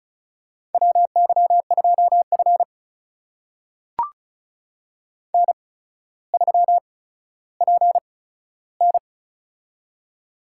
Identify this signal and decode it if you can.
Morse code, heard as a keyed tone, about 36 wpm, spaced as for 15 wpm, 700 Hz. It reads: WY2F N3PN
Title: USA Call Signs - 4 Characters - 8x Character Spacing - 35wpm